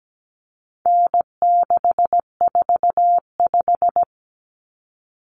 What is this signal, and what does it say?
Morse code, heard as a keyed tone, about 17 wpm, 700 Hz. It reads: N645
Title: Common Cut Numbers - 4-Digit 17wpm